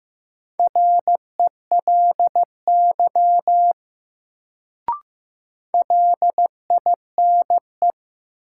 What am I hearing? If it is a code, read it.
Morse code, heard as a keyed tone, about 15 wpm, 700 Hz. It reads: RELY LINE
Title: Sentences from Top 1200 Words - New Word Drill Alt - 15wpm